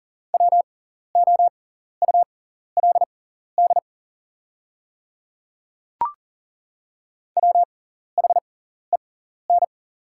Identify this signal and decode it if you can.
Morse code, heard as a keyed tone, about 39 wpm, spaced as for 14 wpm, 690 Hz. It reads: WOULD WHEN
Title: 6x Character Spacing - Top 100 Words - 40wpm